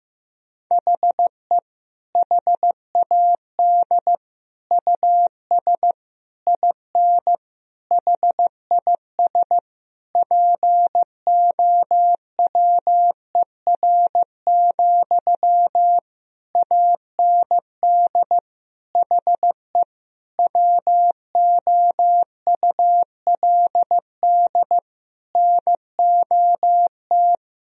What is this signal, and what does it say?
Morse code, heard as a keyed tone, about 15 wpm, 700 Hz. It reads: HE HAD US IN HIS POWER, AND HE WOULD NOT
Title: Sentences from Top 1000 Words - Review Part 2 - 15wpm